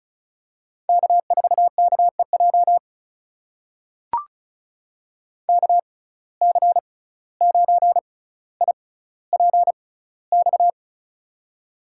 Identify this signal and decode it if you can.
Morse code, heard as a keyed tone, about 34 wpm, spaced as for 17 wpm, 700 Hz. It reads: K4KEJ KC9IPX
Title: USA Call Signs - 6x Character Spacing - 35wpm